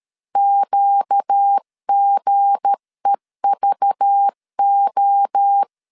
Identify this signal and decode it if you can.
Morse code, heard as a keyed tone, about 13 wpm, 800 Hz. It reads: QGEVO
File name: LEZIONI / 7 GF9Z / 7 G 12wpm 2055